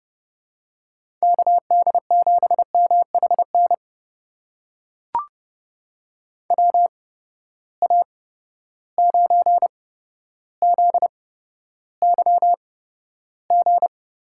Morse code, heard as a keyed tone, about 30 wpm, spaced as for 13 wpm, 700 Hz. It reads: KD7MHN WA9ZYG
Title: USA Call Signs - 6 Characters - 8x Character Spacing - 30wpm